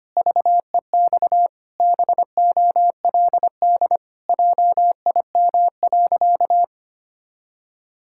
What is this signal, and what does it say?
Morse code, heard as a keyed tone, about 25 wpm, 700 Hz. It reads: VEX BOLD JIM.
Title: Pangram Challenge v2 - 25wpm